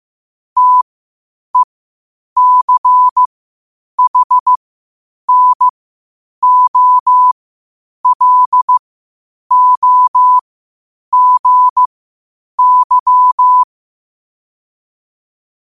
Morse code, heard as a keyed tone, about 15 wpm, spaced as for 8 wpm, 1000 Hz. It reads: TECHNOLOGY
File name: qotdfiles / 8wpm